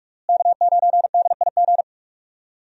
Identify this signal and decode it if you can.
Morse code, heard as a keyed tone, about 47 wpm, 700 Hz. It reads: K9DIG
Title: USA Call Signs - 5 Characters - Repeated 3x - 45wpm